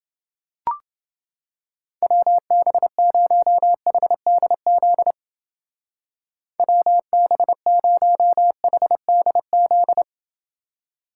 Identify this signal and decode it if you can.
Morse code, heard as a keyed tone, about 30 wpm, 700 Hz. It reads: WB0HDZ WB0HDZ